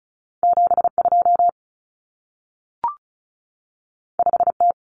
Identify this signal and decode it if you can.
Morse code, heard as a keyed tone, about 36 wpm, 700 Hz. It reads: 72 5T
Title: Common Cut Numbers - 2-Digit 35wpm